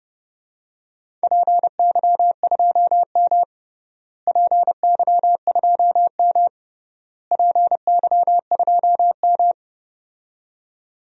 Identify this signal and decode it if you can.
Morse code, heard as a keyed tone, about 30 wpm, 700 Hz. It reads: PY2M PY2M PY2M
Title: DXCC-100 Call Signs - 4 Characters - Repeated 3x - 30wpm